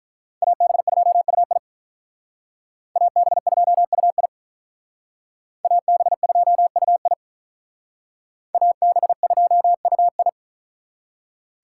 Morse code, heard as a keyed tone, about 45 wpm, 700 Hz. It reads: AB2UI AB2UI AB2UI AB2UI